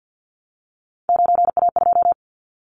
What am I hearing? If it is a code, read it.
Morse code, heard as a keyed tone, about 50 wpm, 700 Hz. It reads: 9A2
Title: Callsign Prefixes for Top 50 DXCC - 50wpm